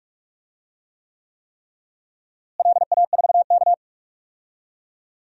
Morse code, heard as a keyed tone, about 46 wpm, 700 Hz. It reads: LA4K
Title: DXCC-100 Call Signs - 4 Characters - Repeated 3x - 45wpm